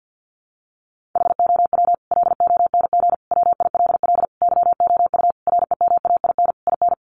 Morse code, heard as a keyed tone, about 50 wpm, 700 Hz. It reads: HOW LONG WILL YOU REMAIN IN
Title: Sentences from Top 1200 Words 50wpm